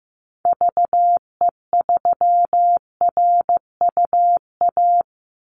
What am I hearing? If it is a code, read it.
Morse code, heard as a keyed tone, about 15 wpm, 700 Hz. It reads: VE3RUA
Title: Call Signs Repeated 2x - 15wpm